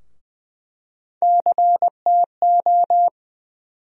Morse code, heard as a keyed tone, about 20 wpm, 700 Hz. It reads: CTO
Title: USA Call Sign Postfixes - 3 Letters 20wpm